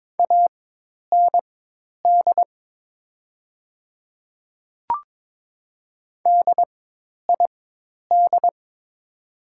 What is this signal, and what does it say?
Morse code, heard as a keyed tone, about 22 wpm, spaced as for 10 wpm, 700 Hz. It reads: AND DID
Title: 4x Character Spacing - Top 100 Words - 22wpm